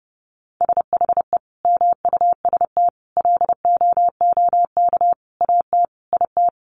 Morse code, heard as a keyed tone, about 30 wpm, 700 Hz. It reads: SHE MUST LOOK AT IT
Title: Sentences from Top 1400 Words 30wpm